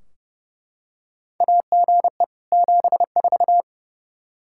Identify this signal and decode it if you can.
Morse code, heard as a keyed tone, about 30 wpm, 700 Hz. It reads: AGE 74
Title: Practice QSO - Exchange 3 - More Info - 30wpm